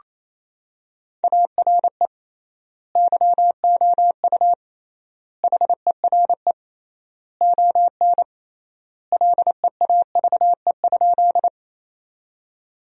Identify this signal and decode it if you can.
Morse code, heard as a keyed tone, about 28 wpm, 700 Hz. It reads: ARE YOU HERE ON LEAVE?
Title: 3x Word-Spacing - Sentences from Top 400 Words - 28wpm